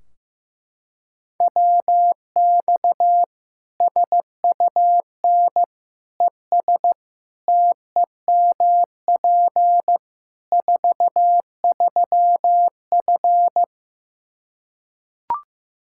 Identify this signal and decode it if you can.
Morse code, heard as a keyed tone, about 15 wpm, 700 Hz. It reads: WX SUN ES TEMP 43F
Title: Practice QSO - Exchange 2 - Info - 15wpm